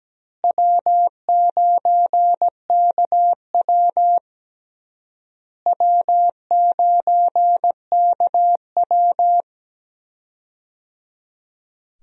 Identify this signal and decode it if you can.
Morse code, heard as a keyed tone, about 17 wpm, 700 Hz. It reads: W9KW W9KW
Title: USA Call Signs - 4 Characters - Repeated 3x - 17wpm